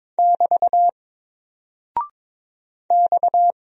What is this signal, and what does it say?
Morse code, heard as a keyed tone, about 22 wpm, 700 Hz. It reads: <BT> X
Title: Single Character (Letter:Number:Punctuation:Prosign) 22wpm